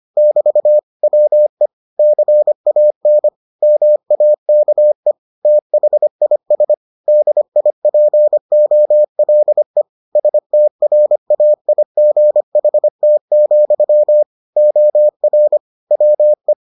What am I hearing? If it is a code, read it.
Morse code, heard as a keyed tone, about 25 wpm, 600 Hz. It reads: <BT> WE CAN MAKE THIS DIPOLE STRAIGHT, OR WE